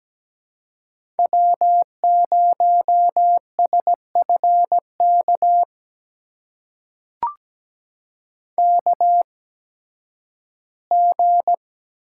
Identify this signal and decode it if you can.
Morse code, heard as a keyed tone, about 17 wpm, spaced as for 11 wpm, 700 Hz. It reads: W0SFK KG
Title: USA Call Signs - 5 Characters - 8x Character Spacing - 17wpm